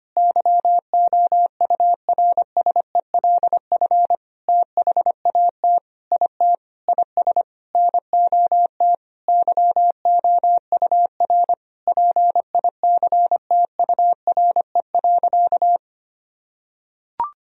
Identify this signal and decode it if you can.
Morse code, heard as a keyed tone, about 25 wpm, 700 Hz. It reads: YOURSELF THAT IT IS NOT YOUR PICTURE.
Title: Sentences from Top 1000 Words - Review Part 1 - 25wpm